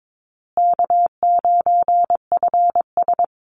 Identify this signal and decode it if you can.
Morse code, heard as a keyed tone, about 22 wpm, 700 Hz. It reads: K9FS
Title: Easy Call Signs Repeated 2x - 22wpm